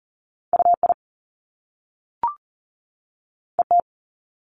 Morse code, heard as a keyed tone, about 38 wpm, 700 Hz. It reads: UI ET
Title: Top Two Letter Combinations in Words 40wpm